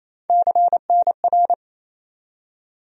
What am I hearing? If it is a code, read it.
Morse code, heard as a keyed tone, about 28 wpm, 700 Hz. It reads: CNR
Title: USA Call Sign Postfixes - 3 Letters 28wpm